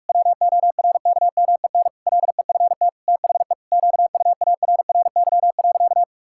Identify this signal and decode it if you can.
Morse code, heard as a keyed tone, about 46 wpm, 700 Hz. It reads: WORKMEN LEFT THE QUARRY.
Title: Pangram Challenge v2 - 45wpm